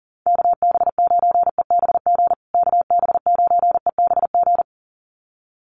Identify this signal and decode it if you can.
Morse code, heard as a keyed tone, about 40 wpm, 700 Hz. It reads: KB9EBG KB9EBG
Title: USA Call Signs Repeated 2x 40wpm